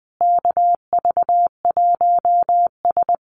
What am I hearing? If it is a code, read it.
Morse code, heard as a keyed tone, about 20 wpm, 700 Hz. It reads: KV1S
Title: USA Call Signs Repeated 2x 20wpm